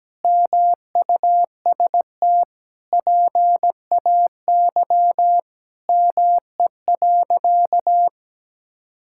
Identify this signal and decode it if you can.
Morse code, heard as a keyed tone, about 17 wpm, 700 Hz. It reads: MUST PAY ME.
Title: Sentences from Top 1000 Words - Review Part 5 - 17wpm